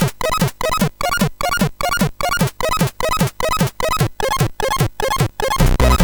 There is no code.